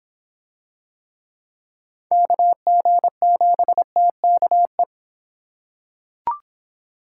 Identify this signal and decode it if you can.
Morse code, heard as a keyed tone, about 26 wpm, 700 Hz. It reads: KG7TKE